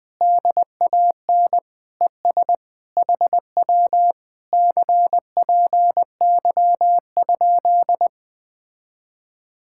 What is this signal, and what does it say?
Morse code, heard as a keyed tone, about 20 wpm, 700 Hz. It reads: DAN ES HW CPY?
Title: Speed-Racing Common QSO Phrases - 17wpm